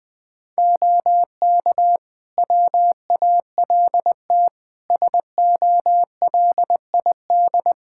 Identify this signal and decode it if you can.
Morse code, heard as a keyed tone, about 21 wpm, 700 Hz. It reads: OK WALT SOLID
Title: Practice QSO - Exchange 3 - More Info - 20wpm